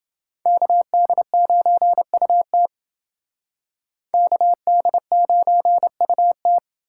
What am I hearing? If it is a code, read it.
Morse code, heard as a keyed tone, about 28 wpm, 700 Hz. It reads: KD9UT KD9UT